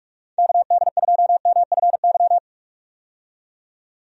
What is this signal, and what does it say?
Morse code, heard as a keyed tone, about 42 wpm, 700 Hz. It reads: KD2MFY
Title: USA Call Signs - 6 Characters - 45wpm